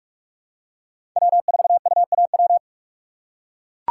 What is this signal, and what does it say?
Morse code, heard as a keyed tone, about 42 wpm, 700 Hz. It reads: W4UAW
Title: USA Call Signs - 5 Characters - 2x Character Spacing - 45wpm